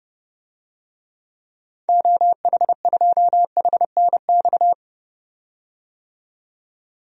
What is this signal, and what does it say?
Morse code, heard as a keyed tone, about 31 wpm, 690 Hz. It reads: OH2HNX
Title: DXCC-100 Call Signs - 6 Characters - 30wpm